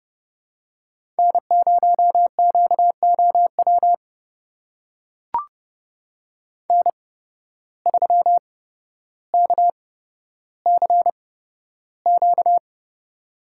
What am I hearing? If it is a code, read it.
Morse code, heard as a keyed tone, about 30 wpm, spaced as for 13 wpm, 700 Hz. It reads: N0QOW N3KCQ
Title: USA Call Signs - 5 Characters - 8x Character Spacing - 30wpm